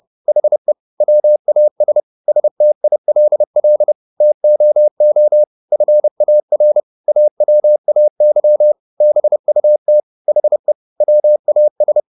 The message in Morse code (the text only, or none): HE WAS STILL TOO FAR AWAY BUT HE WAS